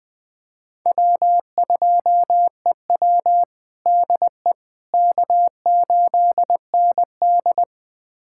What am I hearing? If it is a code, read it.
Morse code, heard as a keyed tone, about 20 wpm, 700 Hz. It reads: W2EW DE K8ND
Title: USA Callsign DE Callsign - 4 Characters - 20wpm